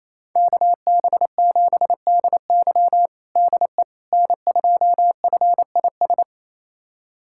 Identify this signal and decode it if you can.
Morse code, heard as a keyed tone, about 28 wpm, 700 Hz. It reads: KB7DY DE N2FIS
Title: USA Callsign DE Callsign - 5 Characters - 28wpm